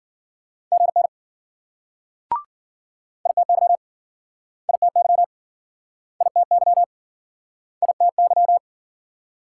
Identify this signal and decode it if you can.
Morse code, heard as a keyed tone, about 57 wpm, 700 Hz. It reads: GN ITY ITY ITY ITY